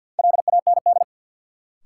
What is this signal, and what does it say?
Morse code, heard as a keyed tone, about 53 wpm, 700 Hz. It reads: LAND
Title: Sentences from Top 500 Words - New Word Drill - 50wpm